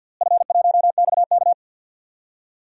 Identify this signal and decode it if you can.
Morse code, heard as a keyed tone, about 49 wpm, 700 Hz. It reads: F1XK
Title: DXCC-100 Call Signs - 4 Characters - Repeated 2x - 50wpm